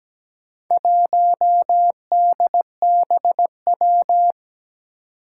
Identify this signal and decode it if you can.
Morse code, heard as a keyed tone, about 17 wpm, 700 Hz. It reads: 1DBW
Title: Four Challenging Letter-Number Combinations 17wpm